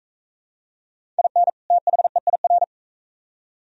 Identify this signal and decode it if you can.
Morse code, heard as a keyed tone, about 41 wpm, 700 Hz. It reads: IN THEIR